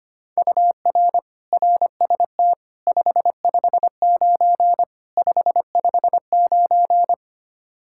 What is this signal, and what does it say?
Morse code, heard as a keyed tone, about 25 wpm, 700 Hz. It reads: UR RST 559 559
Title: Practice QSO - Exchange 1 - Beginning - 25wpm